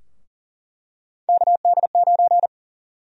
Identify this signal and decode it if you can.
Morse code, heard as a keyed tone, about 40 wpm, 700 Hz. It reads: KD9
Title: USA Call Sign Prefixes - Repeated 2x - 40wpm